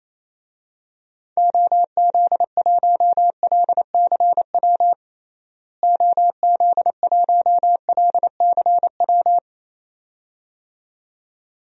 Morse code, heard as a keyed tone, about 27 wpm, 700 Hz. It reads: OZ1LCW OZ1LCW
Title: DXCC-100 Call Signs - 6 Characters - Repeated 2x - 28wpm